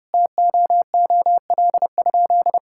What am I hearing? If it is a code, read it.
Morse code, heard as a keyed tone, about 30 wpm, 700 Hz. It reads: TOOL?
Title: Sentences from Top 1000 Words - Review Part 6 - 30wpm